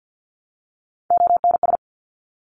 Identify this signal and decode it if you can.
Morse code, heard as a keyed tone, about 50 wpm, 700 Hz. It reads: ONS
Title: Top Three Letter Combinations in Words 50wpm